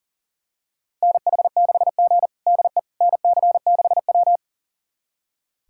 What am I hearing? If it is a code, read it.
Morse code, heard as a keyed tone, about 37 wpm, 700 Hz. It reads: NH6G DE NC6W
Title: USA Callsign DE Callsign - 4 Characters - 40wpm